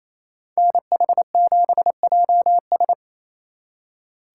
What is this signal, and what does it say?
Morse code, heard as a keyed tone, about 28 wpm, 700 Hz. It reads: NH7JS
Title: Speed Racing - USA 2x2 Call Signs 28wpm